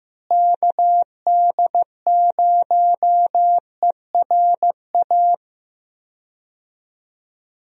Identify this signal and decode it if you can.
Morse code, heard as a keyed tone, about 15 wpm, 700 Hz. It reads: KD0ERA
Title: USA Call Signs - 6 Characters - Repeated 2x - 15wpm